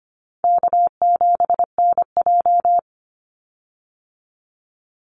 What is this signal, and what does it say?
Morse code, heard as a keyed tone, about 25 wpm, 700 Hz. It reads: K7NJ
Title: Easy Call Signs - 25wpm